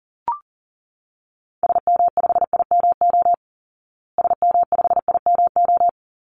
Morse code, heard as a keyed tone, about 40 wpm, 700 Hz. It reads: SM5IMO SM5IMO
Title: Call Signs Repeated 3x - 40wpm